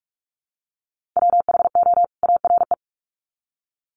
Morse code, heard as a keyed tone, about 44 wpm, 700 Hz. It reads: WHO ARE
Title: Sets of 2 Words 45wpm